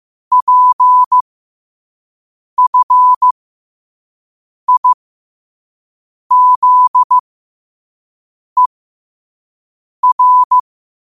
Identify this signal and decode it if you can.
Morse code, heard as a keyed tone, about 15 wpm, spaced as for 5 wpm, 1000 Hz. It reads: PFIZER
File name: qotdfiles / 5wpm